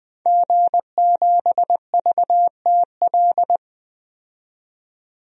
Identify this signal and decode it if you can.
Morse code, heard as a keyed tone, about 20 wpm, 700 Hz. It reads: G7VTL
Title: DXCC-100 Call Signs - 5 Characters - 20wpm